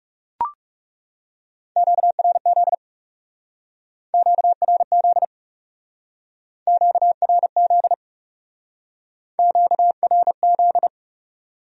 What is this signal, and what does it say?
Morse code, heard as a keyed tone, about 38 wpm, 700 Hz. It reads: QRZ QRZ QRZ QRZ
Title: Speed-Racing Q-Codes - 30wpm